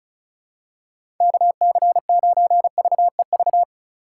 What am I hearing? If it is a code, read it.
Morse code, heard as a keyed tone, about 36 wpm, 700 Hz. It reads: KC9VEV
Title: USA Call Signs - 6 Characters - 2x Character Spacing - 35wpm